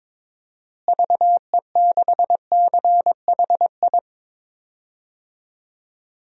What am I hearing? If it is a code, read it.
Morse code, heard as a keyed tone, about 22 wpm, 700 Hz. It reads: VE6CHI
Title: DXCC-100 Call Signs - 6 Characters - Repeated 2x - 22wpm